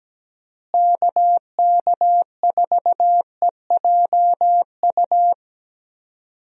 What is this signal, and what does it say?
Morse code, heard as a keyed tone, about 17 wpm, 700 Hz. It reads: KK4EJU
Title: USA Call Signs - 6 Characters - Repeated 3x - 17wpm